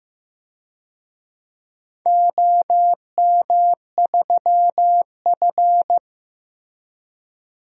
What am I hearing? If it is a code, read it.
Morse code, heard as a keyed tone, about 15 wpm, 700 Hz. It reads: OM3F
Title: DXCC-100 Call Signs - 4 Characters - Repeated 2x - 15wpm